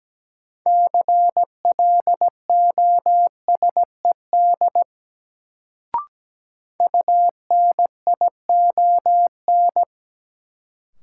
Morse code, heard as a keyed tone, about 17 wpm, 700 Hz. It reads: CLOSED UNION
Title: Sentences from Top 1400 Words - New Word Drill - 17wpm